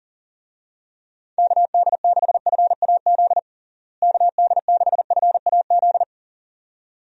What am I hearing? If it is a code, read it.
Morse code, heard as a keyed tone, about 37 wpm, 700 Hz. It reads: KD6FAZ KD6FAZ
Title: USA Call Signs - 6 Characters - Repeated 2x - 40wpm